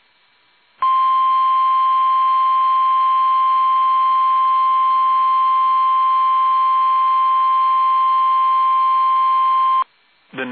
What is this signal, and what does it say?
Emergency Alert System (SAME) tones (the 1050 Hz attention tone), with speech. It is heard from 0.8 to 9.9 s.
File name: eas / KUSO / KUSO